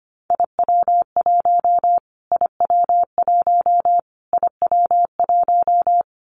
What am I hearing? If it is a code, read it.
Morse code, heard as a keyed tone, about 25 wpm, 700 Hz. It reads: IW1 IW1 IW1
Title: Callsign Prefixes for Top 25 DXCC - Repeated 3x - 25wpm